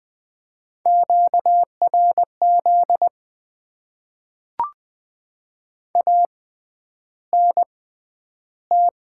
Morse code, heard as a keyed tone, about 20 wpm, spaced as for 10 wpm, 700 Hz. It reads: QRZ ANT
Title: Top 90 QSO Elements - 6x Character Spacing - 20wpm